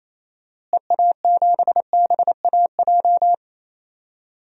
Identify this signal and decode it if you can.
Morse code, heard as a keyed tone, about 27 wpm, 700 Hz. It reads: EA7BAJ
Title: DXCC-100 Call Signs - 6 Characters - Repeated 3x - 28wpm